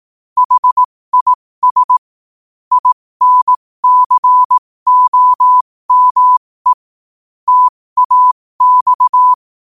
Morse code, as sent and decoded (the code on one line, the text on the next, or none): .... .. ... / .. -. -.-. --- -- . / - .- -..-
HIS INCOME TAX